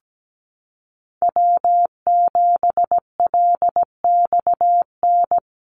Morse code, heard as a keyed tone, about 17 wpm, 700 Hz. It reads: W7LXN
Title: Call Signs Repeated 3x - 17wpm